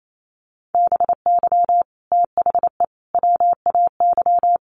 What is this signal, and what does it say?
Morse code, heard as a keyed tone, about 28 wpm, 700 Hz. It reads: BY THE WAY
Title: Sets of 3 Words 28wpm